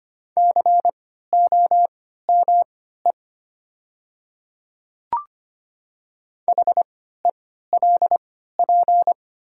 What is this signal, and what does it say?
Morse code, heard as a keyed tone, about 25 wpm, spaced as for 14 wpm, 700 Hz. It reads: COME HELP
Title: 3x Character Spacing - Top 300 Words - 25wpm